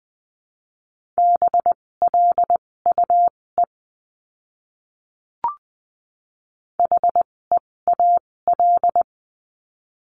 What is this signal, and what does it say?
Morse code, heard as a keyed tone, about 20 wpm, spaced as for 16 wpm, 700 Hz. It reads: BLUE HEAL
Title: 256 Lesson 42a - Words 12-20wpm